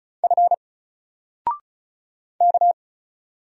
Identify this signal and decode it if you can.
Morse code, heard as a keyed tone, about 36 wpm, 700 Hz. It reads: F K
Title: Single Letter-Number Alt 35wpm